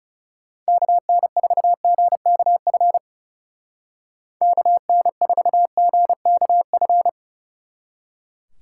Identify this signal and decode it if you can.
Morse code, heard as a keyed tone, about 38 wpm, 700 Hz. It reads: KN4GKF KN4GKF